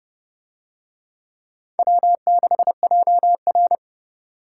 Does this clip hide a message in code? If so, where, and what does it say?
Yes: Morse code from 1.8 to 3.8 s, heard as a keyed tone, about 28 wpm, 700 Hz. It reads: W6JR